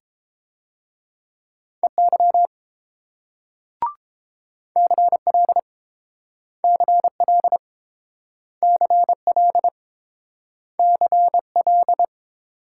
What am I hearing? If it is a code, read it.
Morse code, heard as a keyed tone, about 30 wpm, 690 Hz. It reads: EY CL CL CL CL